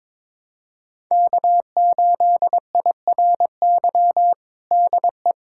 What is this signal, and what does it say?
Morse code, heard as a keyed tone, about 21 wpm, 700 Hz. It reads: K8IRY DE
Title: USA Callsign DE Callsign - 5 Characters - 22wpm